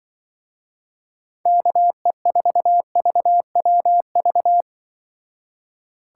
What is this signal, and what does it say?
Morse code, heard as a keyed tone, about 23 wpm, 700 Hz. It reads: KE4VWV
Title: Speed Racing - USA Call Signs 20wpm